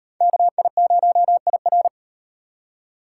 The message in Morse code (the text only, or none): KI0IR